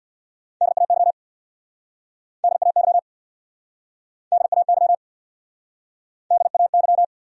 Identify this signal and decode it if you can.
Morse code, heard as a keyed tone, about 69 wpm, 700 Hz. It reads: DAY DAY DAY DAY